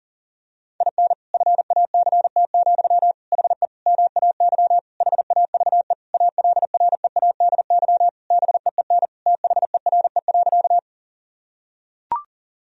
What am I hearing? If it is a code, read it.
Morse code, heard as a keyed tone, about 40 wpm, 700 Hz. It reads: IN FACT, HE MAY HAVE ALREADY BEEN THERE.
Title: Sentences from Top 1000 Words - Review Part 3 - 40wpm